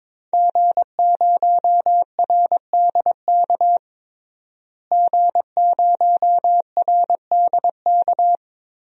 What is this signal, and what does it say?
Morse code, heard as a keyed tone, about 21 wpm, 700 Hz. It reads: G0RDK G0RDK
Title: DXCC-100 Call Signs - 5 Characters - Repeated 2x - 22wpm